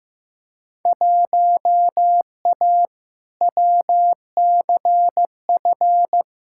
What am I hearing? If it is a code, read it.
Morse code, heard as a keyed tone, about 15 wpm, 700 Hz. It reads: 1A WCF
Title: Field Day Exchanges - 15wpm